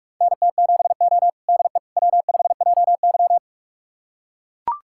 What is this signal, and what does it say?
Morse code, heard as a keyed tone, about 44 wpm, 700 Hz. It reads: NT7O DE W5JY
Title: USA Callsign DE Callsign - 4 Characters - 45wpm